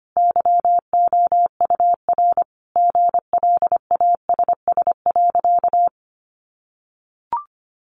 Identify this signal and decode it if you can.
Morse code, heard as a keyed tone, about 25 wpm, 700 Hz. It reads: YOUR GLASS.
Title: Sentences from Top 1300 Words 25wpm